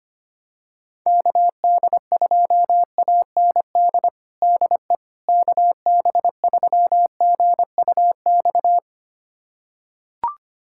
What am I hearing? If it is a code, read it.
Morse code, heard as a keyed tone, about 25 wpm, 700 Hz. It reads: KD2AND DE KB3GUX
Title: USA Callsign DE Callsign - 6 Characters - 25wpm